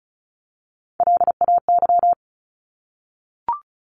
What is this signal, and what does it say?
Morse code, heard as a keyed tone, about 34 wpm, 700 Hz. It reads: LAY
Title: Top 500 Words 35wpm